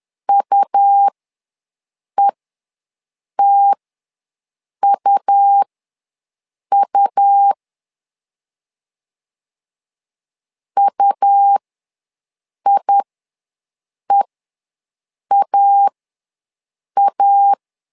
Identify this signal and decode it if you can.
Morse code, heard as a keyed tone, about 11 wpm, spaced as for 6 wpm, 800 Hz. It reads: UETUU UIEAA
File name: LEZIONI / 2 UHDN / 2 U ws 10wpm 2017